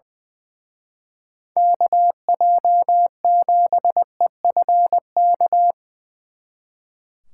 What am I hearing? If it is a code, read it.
Morse code, heard as a keyed tone, about 23 wpm, 700 Hz. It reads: KJ7EFK
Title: Speed Racing - USA 2x3 Call Signs 20wpm